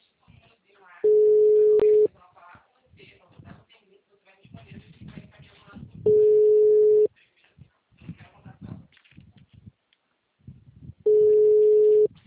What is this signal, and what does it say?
Ringback tone (Europe, 425 Hz), with speech.
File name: out- -2051- - - .22962